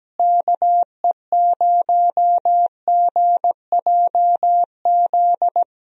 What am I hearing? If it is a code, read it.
Morse code, heard as a keyed tone, about 17 wpm, 700 Hz. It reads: KE0GJZ
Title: USA Call Signs - 6 Characters - Repeated 2x - 17wpm